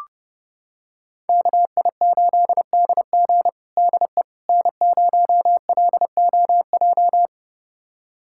Morse code, heard as a keyed tone, about 30 wpm, 700 Hz. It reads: KI8DG DE N0LOJ
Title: USA Callsign DE Callsign - 5 Characters - 30wpm